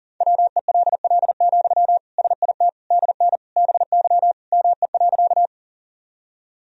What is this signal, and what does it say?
Morse code, heard as a keyed tone, about 37 wpm, 700 Hz. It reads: WELL, SIT DN BY ME.
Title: Sentences from Top 500 Words and Abbreviations 40wpm